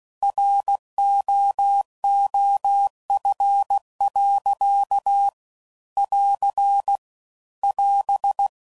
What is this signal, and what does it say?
Morse code, heard as a keyed tone, about 16 wpm, 800 Hz. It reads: ROOF. <AR> <AS>
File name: CW Code Groups @15WPM / Weather Sets / Weather 4 24